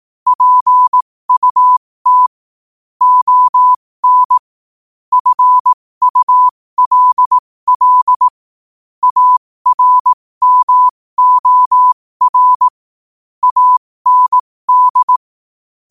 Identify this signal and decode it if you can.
Morse code, heard as a keyed tone, about 18 wpm, spaced as for 15 wpm, 1000 Hz. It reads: PUT ON FULL ARMOR AND
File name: qotdfiles / 15wpm